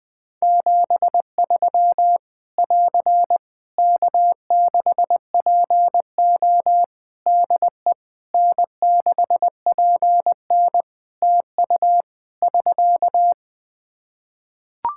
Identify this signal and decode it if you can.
Morse code, heard as a keyed tone, about 20 wpm, 700 Hz. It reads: 73 <AR> K6PO DE N6PN TU <SK>
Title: K1USN SST QSO Practice 20wpm